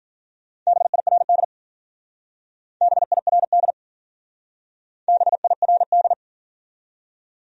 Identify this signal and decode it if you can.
Morse code, heard as a keyed tone, about 49 wpm, 700 Hz. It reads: BIRD BIRD BIRD
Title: Speed-Racing Top 500 Words 40wpm